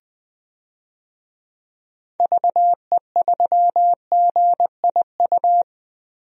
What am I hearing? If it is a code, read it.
Morse code, heard as a keyed tone, about 20 wpm, 700 Hz. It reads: VE3GIU
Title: DXCC-100 Call Signs - 6 Characters - Repeated 3x - 20wpm